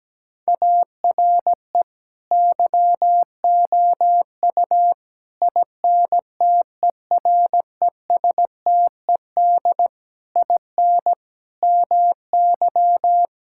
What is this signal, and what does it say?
Morse code, heard as a keyed tone, about 17 wpm, 700 Hz. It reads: ARE YOU INTERESTED IN MY
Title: Sentences from Top 1000 Words - Review Part 6 - 17wpm